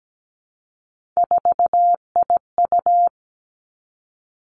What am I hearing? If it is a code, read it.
Morse code, heard as a keyed tone, about 17 wpm, 700 Hz. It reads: 4IU
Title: Three Challenging Letter-Number Combinations 17wpm